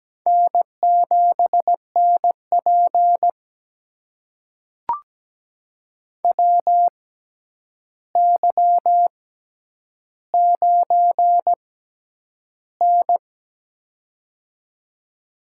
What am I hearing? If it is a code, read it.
Morse code, heard as a keyed tone, about 17 wpm, spaced as for 9 wpm, 700 Hz. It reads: N7NP WY9N
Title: USA Call Signs - 4 Characters - 6x Character Spacing - 17wpm